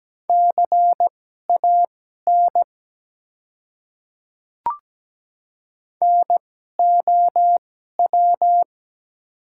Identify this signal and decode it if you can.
Morse code, heard as a keyed tone, about 17 wpm, spaced as for 12 wpm, 700 Hz. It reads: CAN NOW
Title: 2x Character Spacing - Top 100 Words - 17wpm